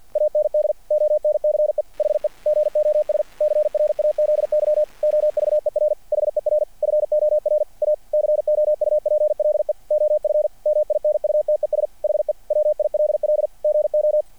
Morse code, heard as a keyed tone, about 50 wpm, 600 Hz, with radio noise. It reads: AND ONCE HE GOS CRAZY OVER HER FOR A COUPLE OF MINUTES HE WILL GO
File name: archived / CRSnet